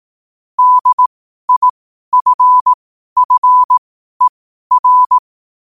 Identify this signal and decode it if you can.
Morse code, heard as a keyed tone, about 18 wpm, spaced as for 12 wpm, 1000 Hz. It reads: DIFFER
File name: qotdfiles / 12wpm